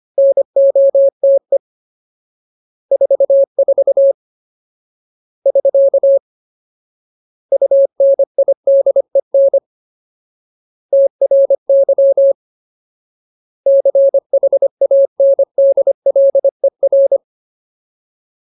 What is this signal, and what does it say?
Morse code, heard as a keyed tone, about 25 wpm, 550 Hz. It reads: NOTE 44 <SK> UNIDEN TRY CHANDLER
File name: Ultimate Word List / 25WPM Ultimate